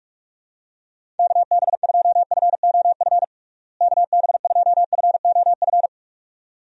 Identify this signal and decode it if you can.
Morse code, heard as a keyed tone, about 46 wpm, 700 Hz. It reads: KB2FOF KB2FOF
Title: USA Call Signs - 6 Characters - Repeated 2x - 45wpm